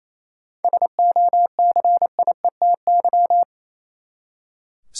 Morse code, heard as a keyed tone, about 28 wpm, 700 Hz. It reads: SOCIETY
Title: Seven Letter Words - 28wpm